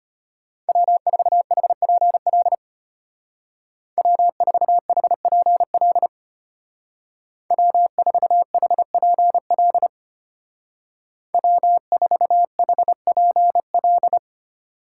Morse code, heard as a keyed tone, about 31 wpm, 700 Hz. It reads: W4HPL W4HPL W4HPL W4HPL
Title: Speed Racing - USA 1x3 Call Signs 25wpm